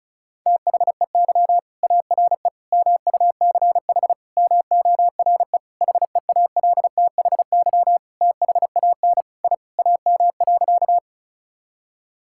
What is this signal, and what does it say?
Morse code, heard as a keyed tone, about 36 wpm, 700 Hz. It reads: THEY ARE MUCH MORE HEALTHY THAN I AM.
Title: Sentences from Top 1000 Words - Review Part 5 - 35wpm